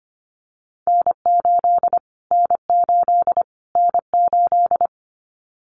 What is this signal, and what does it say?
Morse code, heard as a keyed tone, about 25 wpm, 700 Hz. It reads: N8 N8 N8
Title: Callsign Prefixes for Top 25 DXCC - Repeated 3x - 25wpm